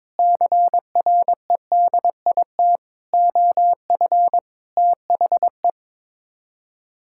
Morse code, heard as a keyed tone, about 22 wpm, 700 Hz. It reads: CREDIT OF THE
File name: practice / 22wpm / Sets of 7 Words - 22wpm